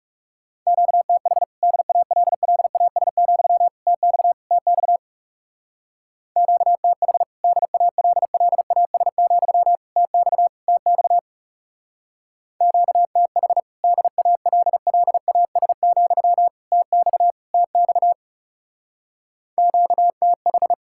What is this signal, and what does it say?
Morse code, heard as a keyed tone, about 36 wpm, 700 Hz. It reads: QTH DALLAS, TX TX QTH DALLAS, TX TX QTH DALLAS, TX TX QTH
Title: Speed-Racing Common QSO Phrases Encore - 30wpm